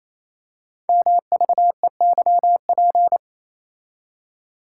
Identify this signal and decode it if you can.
Morse code, heard as a keyed tone, about 28 wpm, 700 Hz. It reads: MVEYP
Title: Code Groups - 5 Letters - 28wpm